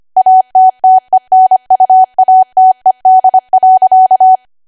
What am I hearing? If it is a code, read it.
Morse code, heard as a keyed tone, about 25 wpm, 750 Hz. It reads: ATTENUATED.
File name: Archive / 25 WPM / 25WPM